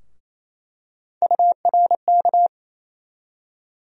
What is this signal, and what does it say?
Morse code, heard as a keyed tone, about 27 wpm, 700 Hz. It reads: URK
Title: USA Call Sign Postfixes - 3 Letters 28wpm